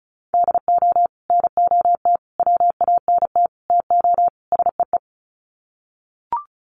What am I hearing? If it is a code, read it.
Morse code, heard as a keyed tone, about 34 wpm, 700 Hz. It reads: DO NOT WANT TO SEE
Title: Sets of 5 Words 35wpm